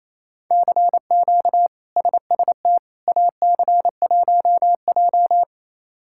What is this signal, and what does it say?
Morse code, heard as a keyed tone, about 28 wpm, 700 Hz. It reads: CQ SST AC1J
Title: K1USN SST QSO Practice - 28wpm